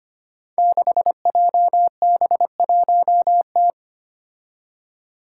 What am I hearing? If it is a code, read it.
Morse code, heard as a keyed tone, about 25 wpm, 700 Hz. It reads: 6JB1T
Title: Five Challenging Letter-Number Combinations 25wpm